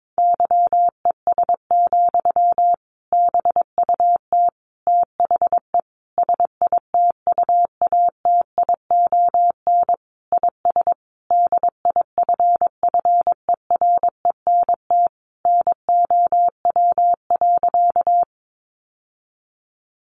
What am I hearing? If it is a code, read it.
Morse code, heard as a keyed tone, about 22 wpm, 700 Hz. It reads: YES, BUT THE SITUATION IS DIFFERENT NOW.
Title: Sentences from Top 1200 Words 22wpm